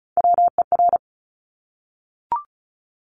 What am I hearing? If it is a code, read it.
Morse code, heard as a keyed tone, about 36 wpm, 700 Hz. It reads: WER
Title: Top Three Letter Combinations in Words 35wpm